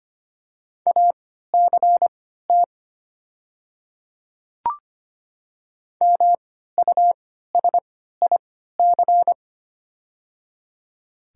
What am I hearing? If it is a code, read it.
Morse code, heard as a keyed tone, about 25 wpm, spaced as for 14 wpm, 700 Hz. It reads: ACT MUSIC
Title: 3x Character Spacing - Top 300 Words - 25wpm